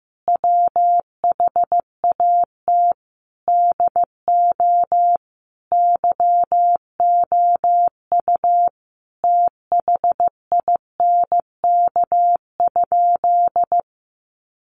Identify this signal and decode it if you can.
Morse code, heard as a keyed tone, about 15 wpm, 700 Hz. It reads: WHAT DO YOU THINK?
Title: Sentences from Top 200 Words 15wpm